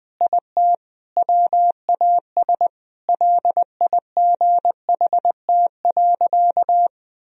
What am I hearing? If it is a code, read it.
Morse code, heard as a keyed tone, about 20 wpm, 700 Hz. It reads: IT WAS LIGHT.
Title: Sentences from Top 1000 Words - Review Part 1 - 20wpm